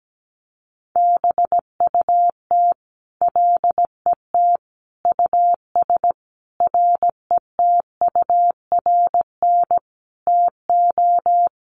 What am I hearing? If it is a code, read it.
Morse code, heard as a keyed tone, about 17 wpm, 700 Hz. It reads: BUT LET US RETURN TO
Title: Sentences from Top 800 Words 17wpm